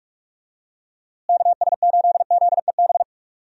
Morse code, heard as a keyed tone, about 46 wpm, 700 Hz. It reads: KS8ZEB
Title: USA Call Signs - 6 Characters - Repeated 3x - 45wpm